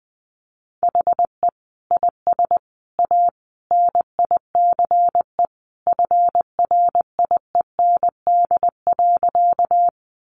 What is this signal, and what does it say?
Morse code, heard as a keyed tone, about 19 wpm, 700 Hz. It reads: HE IS A NICE FRIEND.
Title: Sentences from Top 1100 Words 20wpm